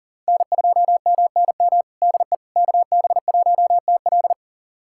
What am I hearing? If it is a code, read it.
Morse code, heard as a keyed tone, about 40 wpm, 700 Hz. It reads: N2MNM DE KB1TL
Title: USA Callsign DE Callsign - 5 Characters - 40wpm